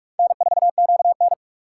Morse code, heard as a keyed tone, about 48 wpm, 700 Hz. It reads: N4QN